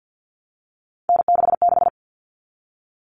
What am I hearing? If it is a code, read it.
Morse code, heard as a keyed tone, about 50 wpm, 700 Hz. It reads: N66
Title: All Cut Numbers - 3-Digit 50wpm